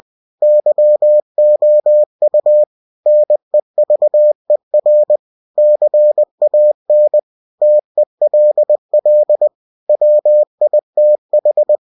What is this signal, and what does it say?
Morse code, heard as a keyed tone, about 20 wpm, 600 Hz. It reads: YOU NEVER CAN TELL WITH
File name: cw-books / winnie-the-pooh / winnie-20wpm-page-0006